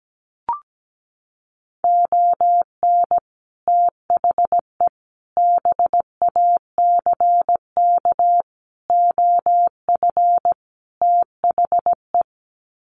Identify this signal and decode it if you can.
Morse code, heard as a keyed tone, about 17 wpm, 700 Hz. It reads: ON THE BACK OF THE
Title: Sets of 5 Words 17wpm